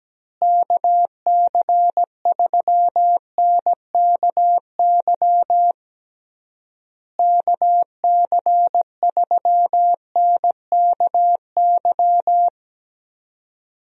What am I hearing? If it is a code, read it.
Morse code, heard as a keyed tone, about 17 wpm, 700 Hz. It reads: KC3NKY KC3NKY
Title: USA Call Signs - 6 Characters - Repeated 3x - 17wpm